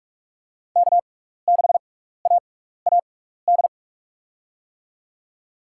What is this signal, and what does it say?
Morse code, heard as a keyed tone, about 46 wpm, 700 Hz. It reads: K 6 A A D
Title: USA Call Signs - 5 Characters - 6x Character Spacing - 45wpm